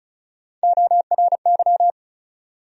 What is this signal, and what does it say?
Morse code, heard as a keyed tone, about 34 wpm, 700 Hz. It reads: ORY
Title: USA Call Sign Postfixes - 3 Letters 35wpm